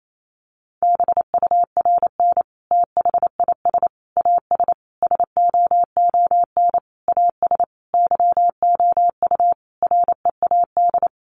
Morse code, heard as a keyed tone, about 28 wpm, 700 Hz. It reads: BURN THIS AS SOON AS YOU READ
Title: Sentences from Top 1300 Words 28wpm